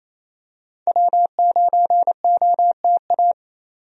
Morse code, heard as a keyed tone, about 28 wpm, 700 Hz. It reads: W9OTA
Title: USA Call Signs - 5 Characters - 28wpm